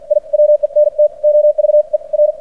Morse code, heard as a keyed tone, about 50 wpm, spaced as for 42 wpm, 600 Hz. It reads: I WENT OVER
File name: archived / CRSnet